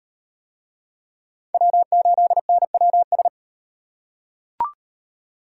Morse code, heard as a keyed tone, about 36 wpm, 700 Hz. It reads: W8NWS